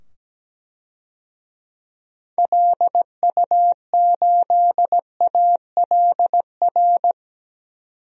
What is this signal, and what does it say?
Morse code, heard as a keyed tone, about 17 wpm, 700 Hz. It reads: LU8ALR
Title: DXCC-100 Call Signs - 6 Characters - 17wpm